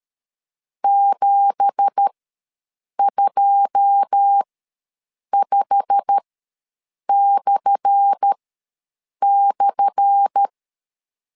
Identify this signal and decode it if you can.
Morse code, heard as a keyed tone, about 13 wpm, spaced as for 7 wpm, 800 Hz. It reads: 725//
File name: LEZIONI / 10 SIGNS / 10A SLASH ws 12wpm 3443